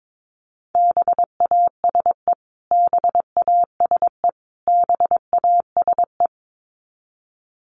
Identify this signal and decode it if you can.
Morse code, heard as a keyed tone, about 22 wpm, 700 Hz. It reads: BASE BASE BASE
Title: Top 500 Words Repeated 22wpm